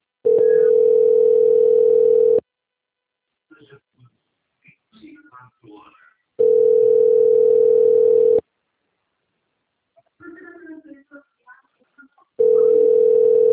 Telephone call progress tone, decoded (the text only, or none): ringback tone